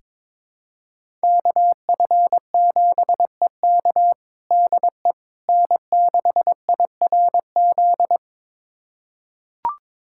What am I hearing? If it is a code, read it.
Morse code, heard as a keyed tone, about 22 wpm, 700 Hz. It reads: KF7EK DE N6IRZ
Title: USA Callsign DE Callsign - 5 Characters - 22wpm